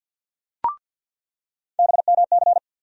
Morse code, heard as a keyed tone, about 48 wpm, 700 Hz. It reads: BMC
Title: USA Call Sign Postfixes - 3 Letters 50wpm